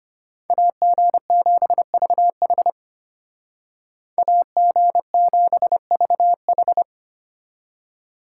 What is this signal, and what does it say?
Morse code, heard as a keyed tone, about 30 wpm, 700 Hz. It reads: AG7VH AG7VH
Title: Speed Racing - USA 2x2 Call Signs 25wpm